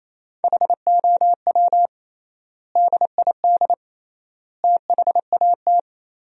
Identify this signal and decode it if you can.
Morse code, heard as a keyed tone, about 28 wpm, 700 Hz. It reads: HOW DID THAT
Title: 3x Word-Spacing - Sentences from Top 200 Words - 28wpm